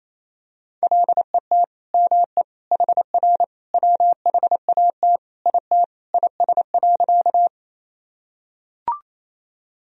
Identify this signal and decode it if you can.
Morse code, heard as a keyed tone, about 27 wpm, 700 Hz. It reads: LET ME HR WHAT IT IS.
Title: Sentences from Top 300 Words and Abbreviations 28wpm